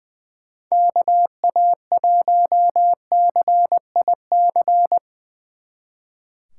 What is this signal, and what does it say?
Morse code, heard as a keyed tone, about 23 wpm, 700 Hz. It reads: KA1CIC
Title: Speed Racing - USA 2x3 Call Signs 20wpm